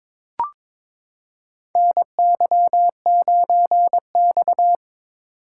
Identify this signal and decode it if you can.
Morse code, heard as a keyed tone, about 21 wpm, 700 Hz. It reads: NY9X
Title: K1USN SST QSO Practice - 22wpm